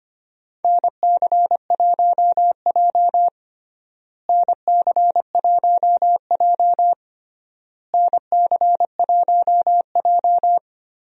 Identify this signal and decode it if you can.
Morse code, heard as a keyed tone, about 25 wpm, 700 Hz. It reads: NC1J NC1J NC1J
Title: USA Call Signs - 4 Characters - Repeated 3x - 25wpm